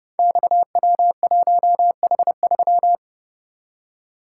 Morse code, heard as a keyed tone, about 30 wpm, 700 Hz. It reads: XW1H3
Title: Code Groups - 5 Letter:Numbers - 30wpm